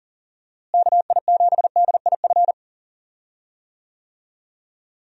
Morse code, heard as a keyed tone, about 40 wpm, 700 Hz. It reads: KI7DIF
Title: USA Call Signs - 6 Characters - Repeated 3x - 40wpm